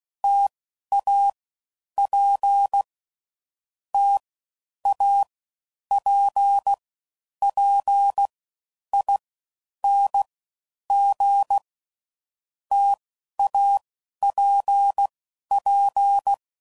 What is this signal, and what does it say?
Morse code, heard as a keyed tone, about 16 wpm, 800 Hz. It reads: T A P T A P P I N G T A P P
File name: Combinations / Difficult Combos / T AOEI's